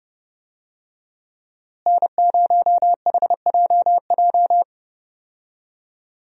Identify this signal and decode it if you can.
Morse code, heard as a keyed tone, about 30 wpm, 700 Hz. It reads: N0HJJ
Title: USA Call Signs - 5 Characters - Repeated 2x - 30wpm